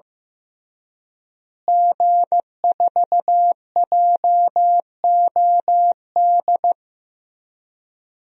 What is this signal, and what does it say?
Morse code, heard as a keyed tone, about 15 wpm, 700 Hz. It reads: G4JOD
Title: DXCC-100 Call Signs - 5 Characters - Repeated 3x - 15wpm